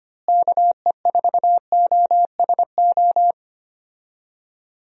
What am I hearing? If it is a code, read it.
Morse code, heard as a keyed tone, about 25 wpm, 700 Hz. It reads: KE4OSO
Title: USA Call Signs - 6 Characters - Repeated 3x - 25wpm